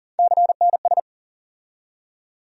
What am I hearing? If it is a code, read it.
Morse code, heard as a keyed tone, about 40 wpm, 700 Hz. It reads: CNS
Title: USA Call Sign Postfixes - 3 Letters 40wpm